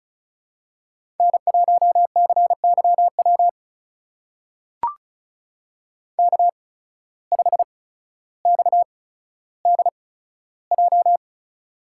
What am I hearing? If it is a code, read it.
Morse code, heard as a keyed tone, about 36 wpm, spaced as for 15 wpm, 700 Hz. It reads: N1CYW K5XDJ
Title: USA Call Signs - 5 Characters - 8x Character Spacing - 35wpm